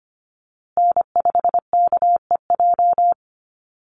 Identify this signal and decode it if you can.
Morse code, heard as a keyed tone, about 25 wpm, 700 Hz. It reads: N5KEJ
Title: USA Call Signs Repeated 2x 25wpm